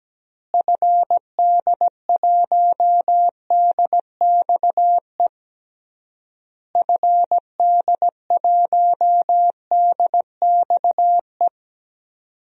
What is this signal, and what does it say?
Morse code, heard as a keyed tone, about 17 wpm, 700 Hz. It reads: FD1DXE FD1DXE
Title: DXCC-100 Call Signs - 6 Characters - Repeated 3x - 17wpm